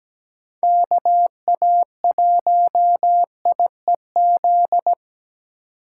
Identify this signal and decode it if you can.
Morse code, heard as a keyed tone, about 17 wpm, 700 Hz. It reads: KA1IEZ
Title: USA Call Signs - 6 Characters - 3x Character Spacing - 17wpm